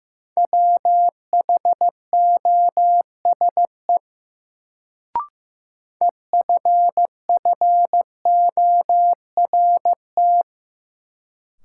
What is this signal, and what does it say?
Morse code, heard as a keyed tone, about 16 wpm, 700 Hz. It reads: WHOSE EFFORT
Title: Sentences from Top 1200 Words - New Word Drill Alt - 15wpm